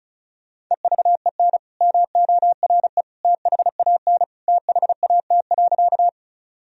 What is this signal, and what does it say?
Morse code, heard as a keyed tone, about 36 wpm, 700 Hz. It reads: EVEN MORE THAN THAT.
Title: Sentences from Top 1000 Words - Review Part 1 - 35wpm